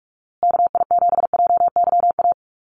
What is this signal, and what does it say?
Morse code, heard as a keyed tone, about 44 wpm, 700 Hz. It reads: KI7JYA
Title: USA Call Signs Repeated 3x 45wpm